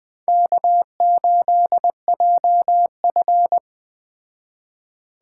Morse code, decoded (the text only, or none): K8JF